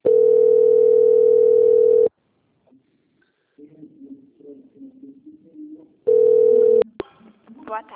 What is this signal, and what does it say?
Ringback tone (North America), 440 + 480 Hz.